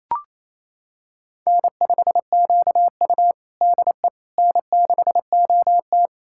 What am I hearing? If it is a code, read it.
Morse code, heard as a keyed tone, about 27 wpm, 690 Hz. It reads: N5QU DE N6OT
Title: USA Callsign DE Callsign - 4 Characters - 28wpm